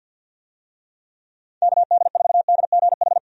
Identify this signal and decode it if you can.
Morse code, heard as a keyed tone, about 51 wpm, 700 Hz. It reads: KD4DGH
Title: USA Call Signs - 6 Characters - 50wpm